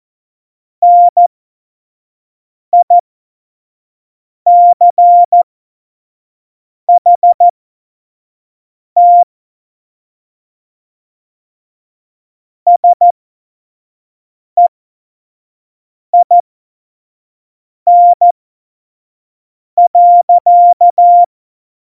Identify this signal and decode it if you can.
Morse code, heard as a keyed tone, about 14 wpm, spaced as for 5 wpm, 700 Hz. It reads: NICHT SEIN.
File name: Morsekurs / Witze / Witz 04. Juni 2019, 25 F70